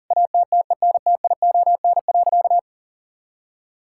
Morse code, heard as a keyed tone, about 40 wpm, 700 Hz. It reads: ATTENTION.
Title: Sentences from Top 1000 Words - Review Part 6 - 40wpm